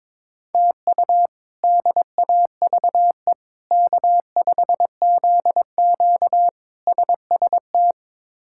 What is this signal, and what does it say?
Morse code, heard as a keyed tone, about 22 wpm, 700 Hz. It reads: TU DAVE K5ZQ SST
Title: K1USN SST QSO Practice - 22wpm